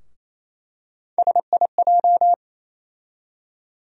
Morse code, heard as a keyed tone, about 28 wpm, 700 Hz. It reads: SIJ
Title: USA Call Sign Postfixes - 3 Letters 28wpm